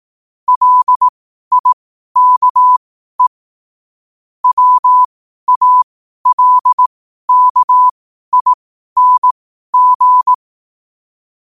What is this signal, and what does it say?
Morse code, heard as a keyed tone, about 18 wpm, spaced as for 12 wpm, 1000 Hz. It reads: LIKE WALKING
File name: qotdfiles / 12wpm